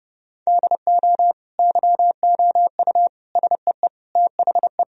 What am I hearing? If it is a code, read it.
Morse code, heard as a keyed tone, about 30 wpm, 700 Hz. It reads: DO YOU SEE THE
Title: Sentences from Top 1000 Words - Review Part 2 - 30wpm